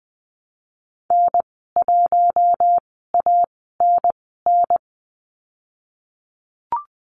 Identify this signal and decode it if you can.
Morse code, heard as a keyed tone, about 20 wpm, spaced as for 15 wpm, 700 Hz. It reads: N1ANN
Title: 055 Lesson 10g - Letter r - Call Signs 20wpm